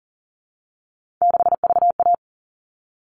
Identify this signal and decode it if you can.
Morse code, heard as a keyed tone, about 40 wpm, 700 Hz. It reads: 6VA
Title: All Cut Numbers - 3-Digit 40wpm